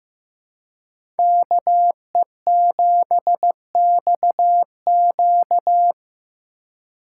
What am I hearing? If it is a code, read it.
Morse code, heard as a keyed tone, about 18 wpm, 700 Hz. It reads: KE7XQ
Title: Speed Racing - USA 2x2 Call Signs 15wpm